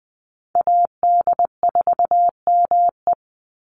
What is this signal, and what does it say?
Morse code, heard as a keyed tone, about 20 wpm, 700 Hz. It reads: AD4ME
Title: USA Call Signs 20wpm